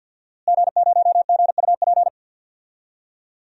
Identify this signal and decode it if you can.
Morse code, heard as a keyed tone, about 50 wpm, 700 Hz. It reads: G0GUP
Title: DXCC-100 Call Signs - 5 Characters - Repeated 3x - 50wpm